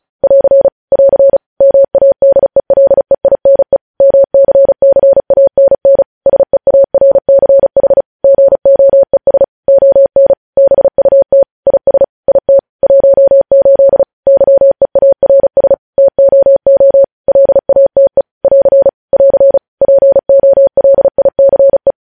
Morse code, heard as a keyed tone, about 36 wpm, 550 Hz. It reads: <AR> <AR> MADELEINE MCCANN SEARCH GOES ON BUT IS IT 18 YEARS TOO LATE <AR> <AR> POLICE
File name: podcast / News headlines-1x-35x35